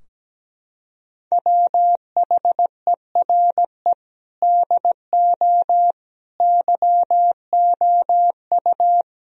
Morse code, heard as a keyed tone, about 17 wpm, 700 Hz. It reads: WHERE DO YOU
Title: Sentences from Top 1000 Words - Review Part 2 - 17wpm